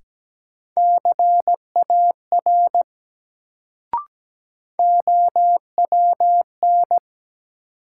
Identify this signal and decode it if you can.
Morse code, heard as a keyed tone, about 17 wpm, 700 Hz. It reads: CAR OWN
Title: Sentences from Top 300 Words - New Word Drill - 17wpm